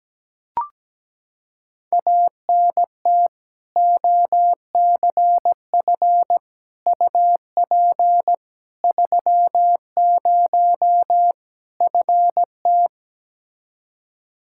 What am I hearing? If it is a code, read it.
Morse code, heard as a keyed tone, about 17 wpm, 700 Hz. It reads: ANT OCF UP 30 FT
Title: Practice QSO - Exchange 2 - Info - 17wpm